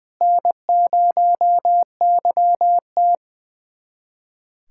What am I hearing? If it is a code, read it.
Morse code, heard as a keyed tone, about 20 wpm, 700 Hz. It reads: N0YT
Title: USA Callsign DE Callsign - 4 Characters - 20wpm